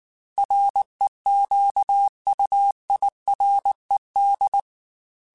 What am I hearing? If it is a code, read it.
Morse code, heard as a keyed tone, about 19 wpm, 800 Hz. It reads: REQUIRED
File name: Technical Subjects / Technical Subjects-mp3s / Sw Pwr Sup @18